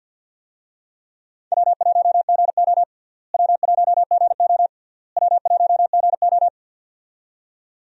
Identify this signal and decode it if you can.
Morse code, heard as a keyed tone, about 53 wpm, 700 Hz. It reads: W1GO W1GO W1GO
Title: USA Call Signs - 4 Characters - Repeated 3x - 50wpm